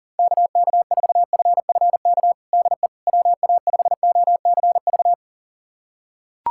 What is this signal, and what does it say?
Morse code, heard as a keyed tone, about 40 wpm, 690 Hz. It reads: KK4FFK DE WA5OCV
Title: USA Callsign DE Callsign - 6 Characters - 40wpm